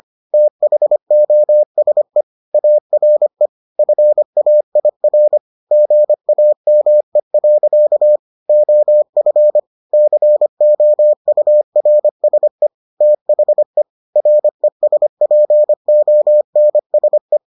Morse code, heard as a keyed tone, about 25 wpm, 600 Hz. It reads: THOSE ARE FAIR GAME. OF COURSE THE RESPONSE